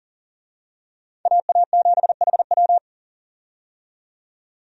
Morse code, heard as a keyed tone, about 40 wpm, 700 Hz. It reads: AA7HW
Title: USA Call Signs - 5 Characters - 40wpm